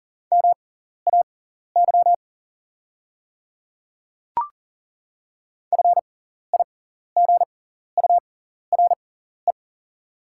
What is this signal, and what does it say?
Morse code, heard as a keyed tone, about 40 wpm, spaced as for 14 wpm, 700 Hz. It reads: MAY FIGURE
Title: 6x Character Spacing - Top 500 Words - 40wpm